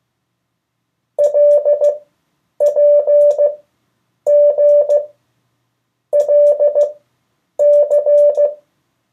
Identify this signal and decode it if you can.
Morse code, heard as a keyed tone, about 16 wpm, spaced as for 8 wpm, 580 Hz. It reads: LPGLC
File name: 1of4-cw-letters